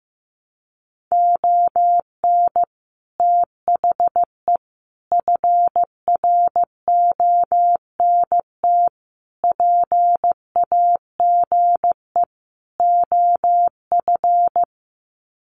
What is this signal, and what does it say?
Morse code, heard as a keyed tone, about 15 wpm, 700 Hz. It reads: ON THE FRONT PAGE OF
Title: Sets of 5 Words Encore 15wpm